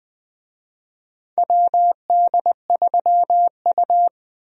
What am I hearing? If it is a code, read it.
Morse code, heard as a keyed tone, about 20 wpm, 700 Hz. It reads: WD3U
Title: USA Call Signs - 4 Characters - 20wpm